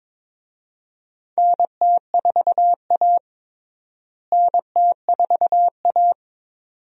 Morse code, heard as a keyed tone, about 22 wpm, 700 Hz. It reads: NT4A NT4A
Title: USA Call Signs - 4 Characters - Repeated 2x - 22wpm